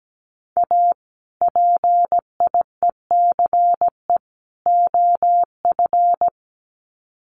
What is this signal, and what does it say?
Morse code, heard as a keyed tone, about 17 wpm, 700 Hz. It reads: A PIECE OF
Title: Sets of 3 Words 17wpm